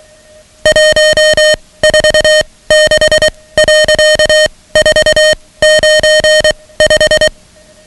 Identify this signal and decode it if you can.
Morse code, heard as a keyed tone, about 23 wpm, spaced as for 17 wpm, 620 Hz, with noise. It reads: 146.495
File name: ham radio / CodeQuick / Code Quick 10